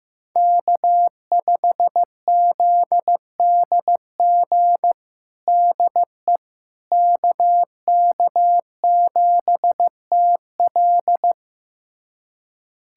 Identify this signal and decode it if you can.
Morse code, heard as a keyed tone, about 15 wpm, 700 Hz. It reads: K5ZDG DE KK7TL
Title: USA Callsign DE Callsign - 5 Characters - 15wpm